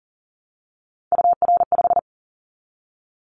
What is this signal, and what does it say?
Morse code, heard as a keyed tone, about 38 wpm, 700 Hz. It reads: UR5
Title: Callsign Prefixes for Top 25 DXCC - 40wpm